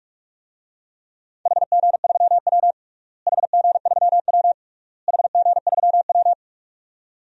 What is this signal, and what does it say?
Morse code, heard as a keyed tone, about 46 wpm, 700 Hz. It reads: HG3W HG3W HG3W
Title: DXCC-100 Call Signs - 4 Characters - Repeated 3x - 45wpm